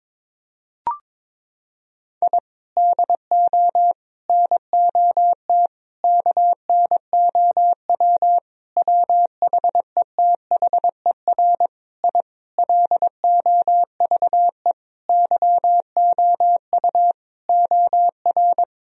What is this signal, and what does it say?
Morse code, heard as a keyed tone, about 21 wpm, 700 Hz. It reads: I DO NOT KNOW WHETHER I LOVE YOU OR
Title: Sentences from Top 1000 Words - Review Part 3 - 22wpm